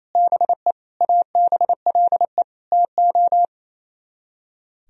Morse code, heard as a keyed tone, about 35 wpm, 700 Hz. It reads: BE ABLE TO